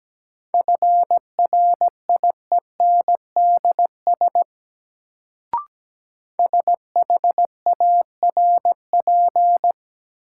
Morse code, heard as a keyed tone, about 17 wpm, 700 Hz. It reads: FRIENDS SHARP
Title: Sentences from Top 1600 Words - New Word Drill Alt - 17wpm